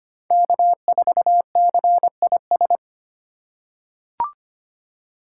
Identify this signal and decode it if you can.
Morse code, heard as a keyed tone, about 25 wpm, 700 Hz. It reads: K4CIS
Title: USA Call Signs - 3x Character Spacing - 25wpm